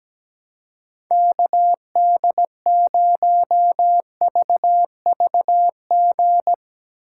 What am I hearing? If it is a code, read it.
Morse code, heard as a keyed tone, about 17 wpm, 700 Hz. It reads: KD0VVG
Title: USA Call Signs - 4x Character Spacing - 17wpm